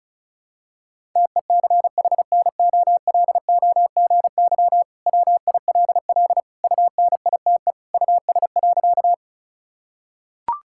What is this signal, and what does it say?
Morse code, heard as a keyed tone, about 34 wpm, 700 Hz. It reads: TECHNOLOGY WILL UNITE US.
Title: Sentences from Top 1000 Words - Review Part 5 - 35wpm